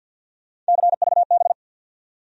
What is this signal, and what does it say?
Morse code, heard as a keyed tone, about 54 wpm, 700 Hz. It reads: CVB
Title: USA Call Sign Postfixes - 3 Letters 50wpm